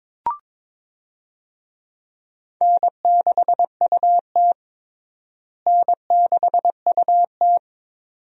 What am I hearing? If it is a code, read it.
Morse code, heard as a keyed tone, about 22 wpm, 700 Hz. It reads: N6UT N6UT
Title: USA Call Signs - 4 Characters - Repeated 2x - 22wpm